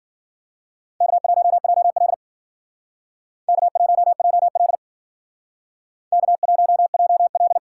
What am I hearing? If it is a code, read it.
Morse code, heard as a keyed tone, about 52 wpm, 700 Hz. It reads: K1JL K1JL K1JL